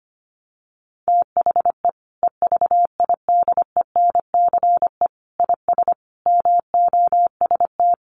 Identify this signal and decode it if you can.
Morse code, heard as a keyed tone, about 25 wpm, 700 Hz. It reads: THE EVIDENCE IS MOST
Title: Sentences from Top 1300 Words 25wpm